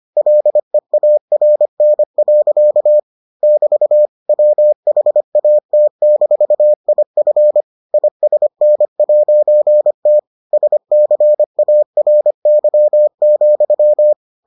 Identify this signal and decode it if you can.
Morse code, heard as a keyed tone, about 25 wpm, 600 Hz. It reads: LEARN. <BT> WHAT-IF ISN'T SCARY,